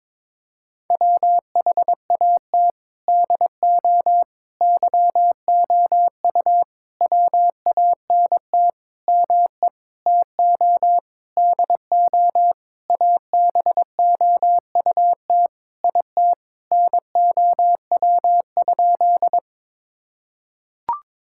Morse code, heard as a keyed tone, about 22 wpm, 700 Hz. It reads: WHAT DO YOU WANT ME TO DO ABOUT IT NOW?
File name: Sentences with 7-Word Phrases - 22wpm